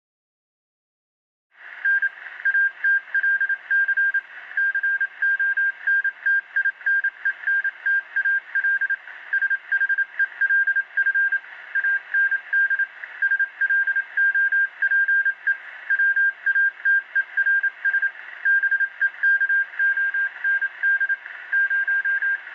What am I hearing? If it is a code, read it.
Morse code, heard as a keyed tone, about 27 wpm, 1600 Hz. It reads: N ATLC CONTINENTAL SHELF AND SLOPE WATERS BEKOND 0